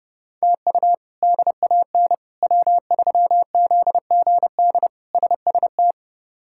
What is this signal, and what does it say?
Morse code, heard as a keyed tone, about 32 wpm, 700 Hz. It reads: TU DAN W3ZGD SST
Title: K1USN SST QSO Practice - 30wpm